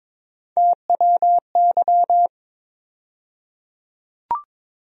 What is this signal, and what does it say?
Morse code, heard as a keyed tone, about 22 wpm, 700 Hz. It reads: TWY
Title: USA Call Sign Postfixes - 3 Letters 22wpm